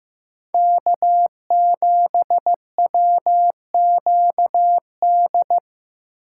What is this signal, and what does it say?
Morse code, heard as a keyed tone, about 15 wpm, 700 Hz. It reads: K7WQD
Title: USA Call Signs - 5 Characters - 2x Character Spacing - 15wpm